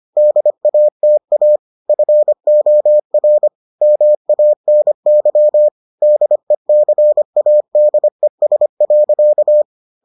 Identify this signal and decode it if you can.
Morse code, heard as a keyed tone, about 25 wpm, 600 Hz. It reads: DATA FOR MANY DECADES.